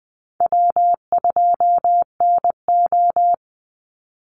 Morse code, heard as a keyed tone, about 20 wpm, 700 Hz. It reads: W2NO
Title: Call Signs - 20wpm